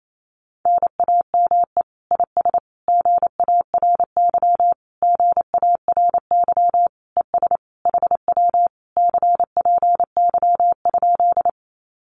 Morse code, heard as a keyed tone, about 28 wpm, 700 Hz. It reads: NAME IS GARY GARY ES HW CPY?
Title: Common QSO Phrases Encore 28wpm